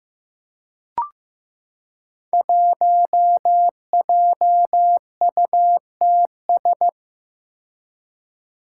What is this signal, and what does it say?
Morse code, heard as a keyed tone, about 15 wpm, 700 Hz. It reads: 1JUTS
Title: Five Challenging Letter-Number Combinations 15wpm